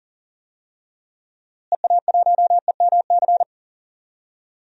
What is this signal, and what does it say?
Morse code, heard as a keyed tone, about 40 wpm, 700 Hz. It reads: EA1EMC
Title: DXCC-100 Call Signs - 6 Characters - Repeated 3x - 40wpm